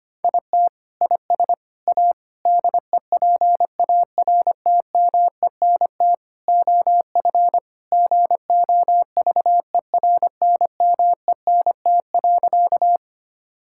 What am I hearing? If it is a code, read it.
Morse code, heard as a keyed tone, about 25 wpm, 700 Hz. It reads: IT IS A DEPARTMENT OF GOVERNMENT.
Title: Sentences from Top 1000 Words - Review Part 5 - 25wpm